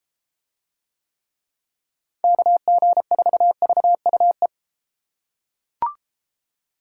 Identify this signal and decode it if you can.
Morse code, heard as a keyed tone, about 32 wpm, 700 Hz. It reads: KG4VUE